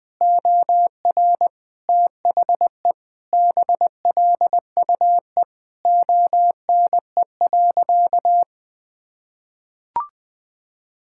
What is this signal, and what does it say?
Morse code, heard as a keyed tone, about 21 wpm, 700 Hz. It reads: OR THE BLUE ONE.
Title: Sentences with 2-Word Phrases - 20wpm